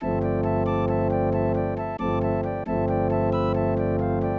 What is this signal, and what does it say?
There is no code.